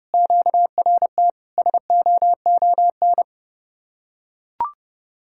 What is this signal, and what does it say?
Morse code, heard as a keyed tone, about 31 wpm, 700 Hz. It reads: QRT SOON